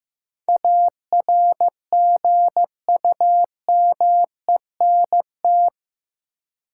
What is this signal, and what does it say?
Morse code, heard as a keyed tone, about 15 wpm, 700 Hz. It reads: ARGUMENT
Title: Sentences from Top 1400 Words - New Word Drill Alt - 15wpm